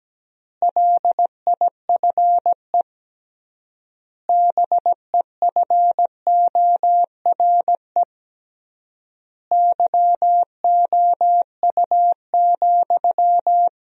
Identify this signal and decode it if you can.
Morse code, heard as a keyed tone, about 17 wpm, 700 Hz. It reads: LIFE BEFORE YOU,
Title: 3x Word-Spacing - Sentences from Top 400 Words - 17wpm